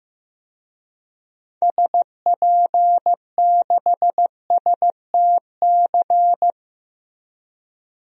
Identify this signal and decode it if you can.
Morse code, heard as a keyed tone, about 15 wpm, 700 Hz. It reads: SP6STC
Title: DXCC-100 Call Signs - 6 Characters - Repeated 3x - 15wpm